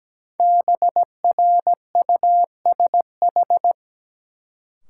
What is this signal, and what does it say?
Morse code, heard as a keyed tone, about 17 wpm, 700 Hz. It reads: BRUSH
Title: Sentences from Top 1100 Words - New Word Drill Alt - 17wpm